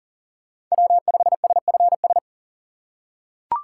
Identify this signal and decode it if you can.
Morse code, heard as a keyed tone, about 40 wpm, 700 Hz. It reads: W5SFS
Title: USA Call Signs - 6x Character Spacing - 40wpm